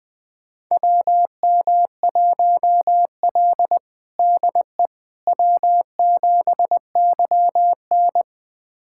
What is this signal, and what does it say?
Morse code, heard as a keyed tone, about 20 wpm, 700 Hz. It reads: WM1L DE W7YN
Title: USA Callsign DE Callsign - 4 Characters - 20wpm